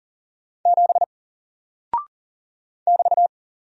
Morse code, heard as a keyed tone, about 40 wpm, 700 Hz. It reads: 7 <BT>
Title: Single Character (Letter:Number:Punctuation:Prosign) 40wpm